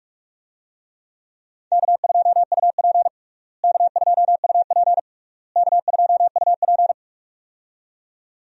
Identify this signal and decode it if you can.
Morse code, heard as a keyed tone, about 46 wpm, 700 Hz. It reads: K2UP K2UP K2UP
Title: USA Call Signs - 4 Characters - Repeated 3x - 45wpm